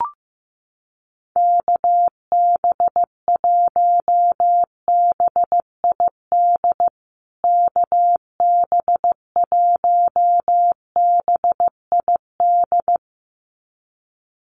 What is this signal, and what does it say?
Morse code, heard as a keyed tone, about 15 wpm, 700 Hz. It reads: KB1BID KB1BID
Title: USA Call Signs Repeated 2x 15wpm